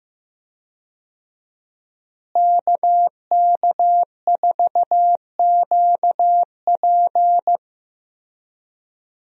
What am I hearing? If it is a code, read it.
Morse code, heard as a keyed tone, about 15 wpm, 700 Hz. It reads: KK4QP
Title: USA Call Signs - 5 Characters - Repeated 3x - 15wpm